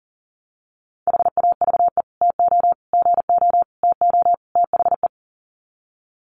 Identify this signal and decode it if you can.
Morse code, heard as a keyed tone, about 40 wpm, 700 Hz. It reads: HAVE TO GO TO THE
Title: Sets of 5 Words Encore 40wpm